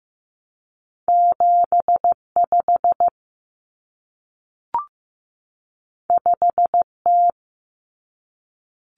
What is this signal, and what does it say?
Morse code, heard as a keyed tone, about 15 wpm, 700 Hz. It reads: 75 5T
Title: Common Cut Numbers - 2-Digit 15wpm